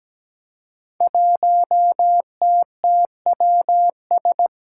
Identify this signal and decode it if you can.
Morse code, heard as a keyed tone, about 17 wpm, 700 Hz. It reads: 1TTWS
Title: Five Challenging Letter-Number Combinations 17wpm